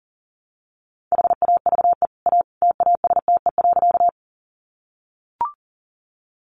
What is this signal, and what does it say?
Morse code, heard as a keyed tone, about 40 wpm, 700 Hz. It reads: HAVE A TASTE.
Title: Sentences from Top 1100 Words 40wpm